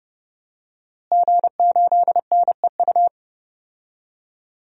Morse code, heard as a keyed tone, about 30 wpm, 700 Hz. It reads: G8NEU
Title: DXCC-100 Call Signs - 5 Characters - Repeated 2x - 30wpm